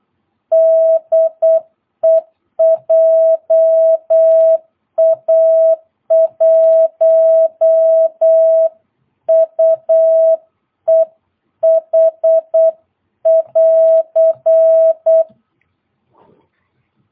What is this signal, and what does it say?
Morse code, heard as a keyed tone, about 8 wpm, 660 Hz. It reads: DE JA1UEH <AR>
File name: jh8cbh / 2022cw